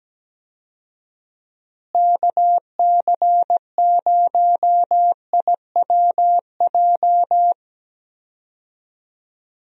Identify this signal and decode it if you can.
Morse code, heard as a keyed tone, about 17 wpm, 700 Hz. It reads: KC0IWJ
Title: USA Call Signs - 6 Characters - Repeated 3x - 17wpm